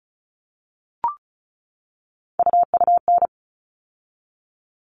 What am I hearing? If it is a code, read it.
Morse code, heard as a keyed tone, about 34 wpm, 700 Hz. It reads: UUN
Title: All Cut Numbers - 3-Digit 35wpm